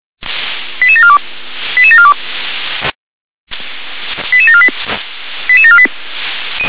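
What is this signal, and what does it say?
Five-tone selective call (ZVEI), heard as quick runs of five tones, tones of about 70 ms. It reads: 88742 88742 88748 88748
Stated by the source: ZVEI five-tone selective call received with an SDR.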